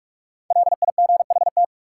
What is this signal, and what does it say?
Morse code, heard as a keyed tone, about 46 wpm, 700 Hz. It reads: LIGHT
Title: Sentences from Top 200 Words - New Word Drill - 45wpm